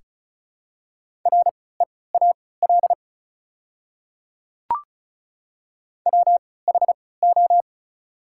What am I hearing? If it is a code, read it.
Morse code, heard as a keyed tone, about 36 wpm, spaced as for 20 wpm, 700 Hz. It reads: REAL WHO
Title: 3x Character Spacing - Top 300 Words - 35wpm